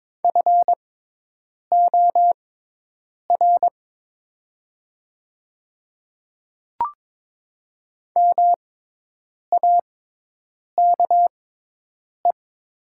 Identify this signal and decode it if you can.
Morse code, heard as a keyed tone, about 22 wpm, spaced as for 8 wpm, 700 Hz. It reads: FOR MAKE
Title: 6x Character Spacing - Top 300 Words - 22wpm